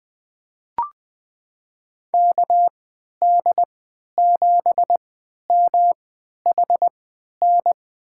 Morse code, heard as a keyed tone, about 20 wpm, spaced as for 11 wpm, 700 Hz. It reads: KD7MHN
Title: USA Call Signs - 3x Character Spacing - 20wpm